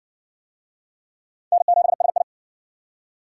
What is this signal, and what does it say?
Morse code, heard as a keyed tone, about 56 wpm, 700 Hz. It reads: N7SI